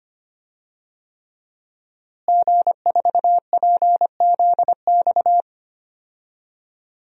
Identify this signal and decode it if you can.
Morse code, heard as a keyed tone, about 25 wpm, 700 Hz. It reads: G4PZX
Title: DXCC-100 Call Signs - 5 Characters - Repeated 3x - 25wpm